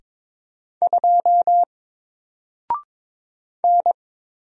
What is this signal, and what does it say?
Morse code, heard as a keyed tone, about 22 wpm, 700 Hz. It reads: 2 N
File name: Single-Letter-Number-22wpm